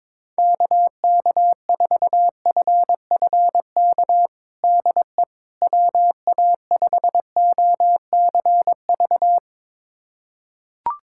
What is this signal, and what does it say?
Morse code, heard as a keyed tone, about 22 wpm, 700 Hz. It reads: KK4FFK DE WA5OCV
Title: USA Callsign DE Callsign - 6 Characters - 22wpm